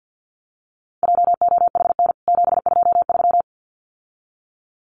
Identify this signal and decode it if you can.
Morse code, heard as a keyed tone, about 50 wpm, 700 Hz. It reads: JOHN 723
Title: CWT Exchange 50wpm